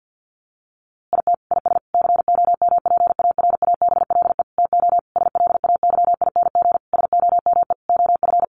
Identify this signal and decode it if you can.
Morse code, heard as a keyed tone, about 48 wpm, 700 Hz. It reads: IT IS COMPARABLE TO SLAYING SOME OF
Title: A Sound Of Thunder 50wpm